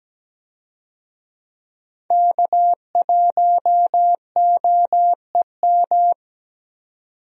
Morse code, heard as a keyed tone, about 17 wpm, 700 Hz. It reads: K1OEM
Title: USA Call Signs - 5 Characters - 17wpm